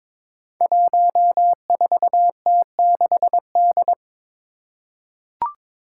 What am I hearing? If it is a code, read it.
Morse code, heard as a keyed tone, about 22 wpm, 700 Hz. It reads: 14T6D
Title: Five Challenging Letter-Number Combinations 22wpm